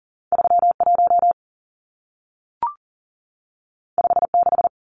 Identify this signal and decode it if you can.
Morse code, heard as a keyed tone, about 40 wpm, 700 Hz. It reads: 31 56
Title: Numbers - 2-Digit 40wpm